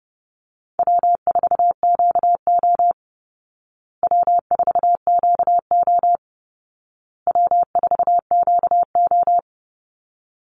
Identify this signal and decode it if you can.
Morse code, heard as a keyed tone, about 30 wpm, 700 Hz. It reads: W4QO W4QO W4QO
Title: Easy Call Signs Repeated 3x - 30wpm